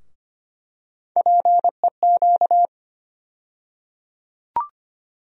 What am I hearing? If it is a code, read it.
Morse code, heard as a keyed tone, about 25 wpm, 700 Hz. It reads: PEQ
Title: USA Call Sign Postfixes - 3 Letters 25wpm